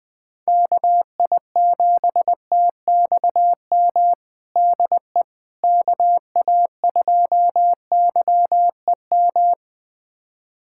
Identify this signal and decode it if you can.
Morse code, heard as a keyed tone, about 20 wpm, 700 Hz. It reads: KI7TXM DE KA2YEM
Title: USA Callsign DE Callsign - 6 Characters - 20wpm